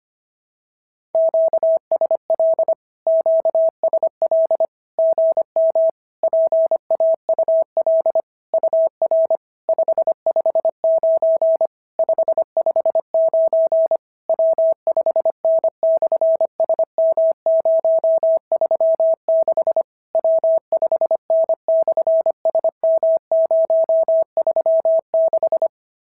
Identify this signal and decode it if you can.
Morse code, heard as a keyed tone, about 25 wpm, 650 Hz. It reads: QSL QSL GM PAUL UR 559 559 W5N/SM036 W5N/SM036
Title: SOTA QSO Practice 25wpm